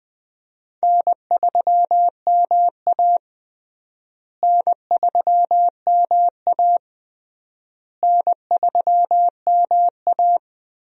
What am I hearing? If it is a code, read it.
Morse code, heard as a keyed tone, about 20 wpm, 700 Hz. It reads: N3MA N3MA N3MA
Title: USA Call Signs - 4 Characters - Repeated 3x - 20wpm